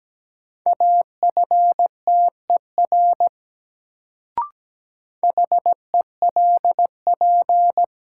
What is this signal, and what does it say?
Morse code, heard as a keyed tone, about 17 wpm, 700 Hz. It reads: AFTER HELP
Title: Sets of 2 Words - New Word Drill - 17wpm